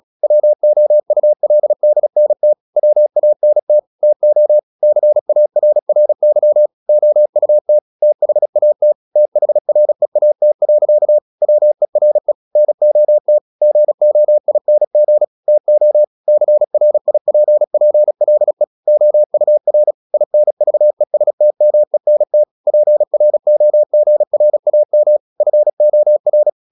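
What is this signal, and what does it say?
Morse code, heard as a keyed tone, about 36 wpm, 600 Hz. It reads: WOULDNT WANT TO CARRY OUT THAT THREAT. WERE NOT GOING TO CRIPPLE OUR INVESTMENT PROGRAM FOR
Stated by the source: Morse practice at 36 wpm, text of Ken Follett's novel Never, from CW med Gnister.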